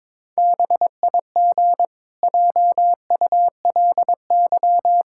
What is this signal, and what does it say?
Morse code, heard as a keyed tone, about 22 wpm, 700 Hz. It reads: BIG JULY
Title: Pangram Challenge v2 - 22wpm